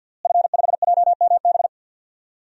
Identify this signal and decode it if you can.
Morse code, heard as a keyed tone, about 52 wpm, 700 Hz. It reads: F5JMB
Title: DXCC-100 Call Signs - 5 Characters - Repeated 2x - 50wpm